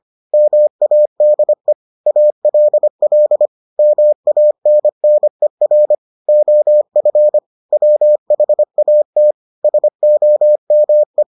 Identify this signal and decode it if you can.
Morse code, heard as a keyed tone, about 25 wpm, 600 Hz. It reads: MADE ALL MANNER OF WHAT SOME